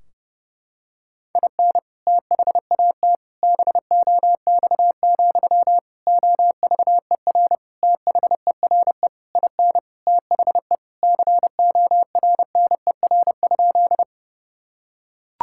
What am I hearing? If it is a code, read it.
Morse code, heard as a keyed tone, about 32 wpm, 700 Hz. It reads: IN THAT BOX, OVER THERE IN THE CORNER?
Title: Sentences from Top 1600 Words 30wpm